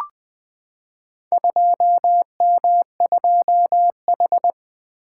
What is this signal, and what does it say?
Morse code, heard as a keyed tone, about 20 wpm, 700 Hz. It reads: 2M2H
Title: Four Letter-Number Combinations - 20wpm